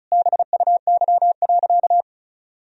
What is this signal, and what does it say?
Morse code, heard as a keyed tone, about 36 wpm, 700 Hz. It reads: BUY.
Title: Sentences from Top 1000 Words - Review Part 4 - 35wpm